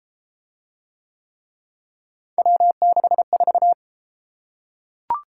Morse code, heard as a keyed tone, about 31 wpm, 700 Hz. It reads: W64